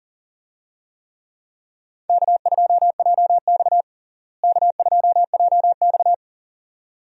Morse code, heard as a keyed tone, about 40 wpm, 700 Hz. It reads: K2JX K2JX
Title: USA Call Signs - 4 Characters - Repeated 2x - 40wpm